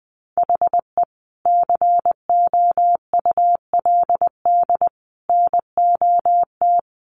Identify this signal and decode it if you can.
Morse code, heard as a keyed tone, about 20 wpm, 700 Hz. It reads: HE COULD NOT
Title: Sets of 3 Words 20wpm